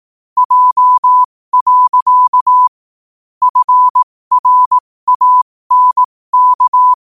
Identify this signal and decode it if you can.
Morse code, heard as a keyed tone, about 18 wpm, spaced as for 15 wpm, 1000 Hz. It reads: J. FRANK
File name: qotdfiles / 15wpm